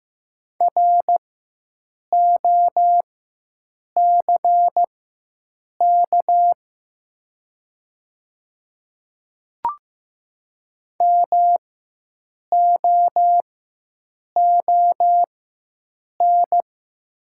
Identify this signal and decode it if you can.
Morse code, heard as a keyed tone, about 15 wpm, spaced as for 7 wpm, 690 Hz. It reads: ROCK MOON
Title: 4x Character Spacing - Top 500 Words - 15wpm